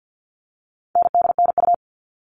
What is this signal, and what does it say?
Morse code, heard as a keyed tone, about 50 wpm, 700 Hz. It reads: NDNU
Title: All Cut Numbers - 4-Digit 50wpm